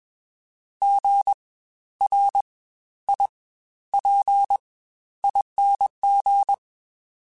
Morse code, heard as a keyed tone, about 21 wpm, spaced as for 12 wpm, 800 Hz. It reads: GRIPING
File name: Spaced Alphas